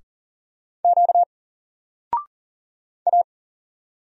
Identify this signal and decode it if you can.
Morse code, heard as a keyed tone, about 40 wpm, 700 Hz. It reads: Q A
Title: Single Letter-Number Alt 40wpm